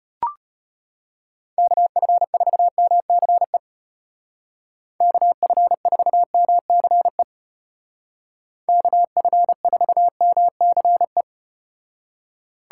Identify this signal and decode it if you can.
Morse code, heard as a keyed tone, about 33 wpm, 700 Hz. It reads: KF4MCE KF4MCE KF4MCE